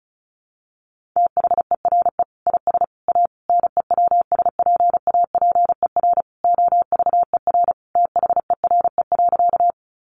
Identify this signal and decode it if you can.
Morse code, heard as a keyed tone, about 34 wpm, 700 Hz. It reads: THERE IS A NEWSPAPER OVER THERE.
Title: Sentences from Top 1100 Words 35wpm